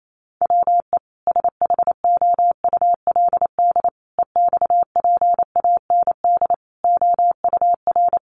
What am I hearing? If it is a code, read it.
Morse code, heard as a keyed tone, about 28 wpm, 700 Hz. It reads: WE SHOULD EXPAND OUR
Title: Sentences from Top 1400 Words 28wpm